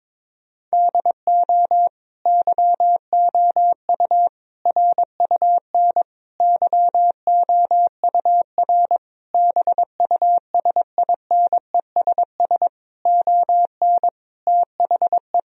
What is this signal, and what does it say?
Morse code, heard as a keyed tone, about 22 wpm, 700 Hz. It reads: DO YOU RUN YOUR BUSINESS ON THE
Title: Sentences from Top 1000 Words - Review Part 4 - 22wpm